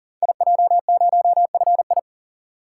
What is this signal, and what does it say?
Morse code, heard as a keyed tone, about 39 wpm, 690 Hz. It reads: IJ0FI
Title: Code Groups - 5 Letter:Numbers - 40wpm